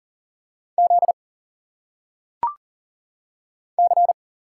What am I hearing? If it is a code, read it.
Morse code, heard as a keyed tone, about 40 wpm, 700 Hz. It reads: Z C
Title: Single Letters - Phonetic Pronunciation 40wpm